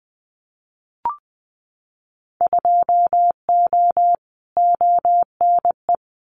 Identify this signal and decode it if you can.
Morse code, heard as a keyed tone, about 19 wpm, 700 Hz. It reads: 2O ONE
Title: Winter Field Day Exchanges Encore 20wpm